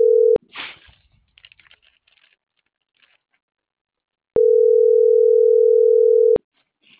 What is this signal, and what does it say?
Ringback tone (North America), 440 + 480 Hz.